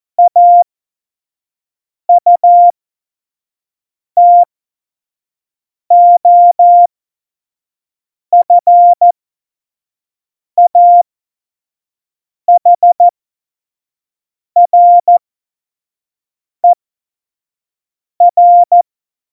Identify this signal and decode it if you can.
Morse code, heard as a keyed tone, about 14 wpm, spaced as for 5 wpm, 700 Hz. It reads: AUTOFAHRER
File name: Morsekurs / Witze / Witz 12. Februar 2019, 25 F70